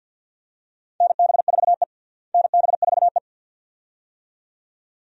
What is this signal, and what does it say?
Morse code, heard as a keyed tone, about 53 wpm, 700 Hz. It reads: NB4E NB4E
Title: USA Call Signs - 4 Characters - Repeated 2x - 50wpm